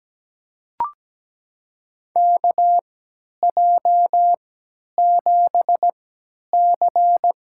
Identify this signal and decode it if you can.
Morse code, heard as a keyed tone, about 17 wpm, spaced as for 10 wpm, 700 Hz. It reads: KJ7C
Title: USA Call Signs - 3x Character Spacing - 17wpm